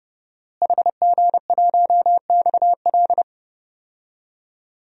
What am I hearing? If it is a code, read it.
Morse code, heard as a keyed tone, about 30 wpm, 700 Hz. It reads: HG1XL
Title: DXCC-100 Call Signs - 5 Characters - Repeated 2x - 30wpm